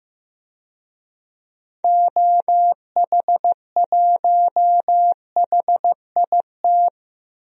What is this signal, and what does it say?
Morse code, heard as a keyed tone, about 15 wpm, 700 Hz. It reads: OH1HIT
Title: DXCC-100 Call Signs - 6 Characters - 15wpm